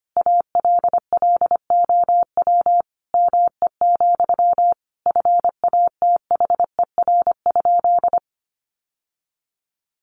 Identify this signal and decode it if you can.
Morse code, heard as a keyed tone, about 25 wpm, 700 Hz. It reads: ALLOW ME, FATHER?
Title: Sentences from Top 700 Words 25wpm